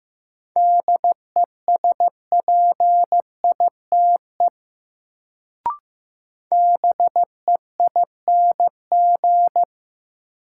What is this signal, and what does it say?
Morse code, heard as a keyed tone, about 15 wpm, 700 Hz. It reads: DESPITE BEING
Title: Sets of 4 Words - New Word Drill - 15wpm